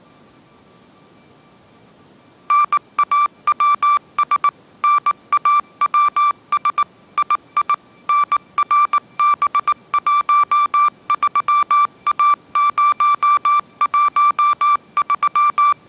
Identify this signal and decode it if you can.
Morse code, heard as a keyed tone, about 19 wpm, 1170 Hz. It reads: NAWS NAWS II NRB13A013
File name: 20wpm